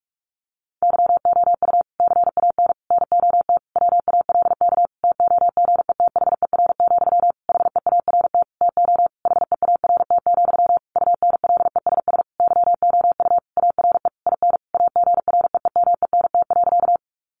Morse code, heard as a keyed tone, about 44 wpm, 700 Hz. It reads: YOU CAN NOT WALK TOGETHER, HEART TO HEART, UNLESS YOU ARE IN AGREEMENT.
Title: Sentences from Top 1500 Words 45wpm